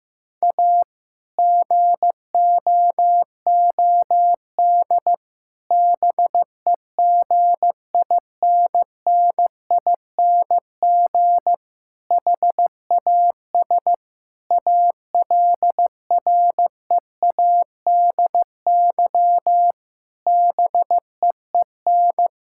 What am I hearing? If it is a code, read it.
Morse code, heard as a keyed tone, about 15 wpm, 700 Hz. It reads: A GOOD BEGINNING HAS ALREADY BEEN
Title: Sentences from Top 1000 Words - Review Part 5 - 15wpm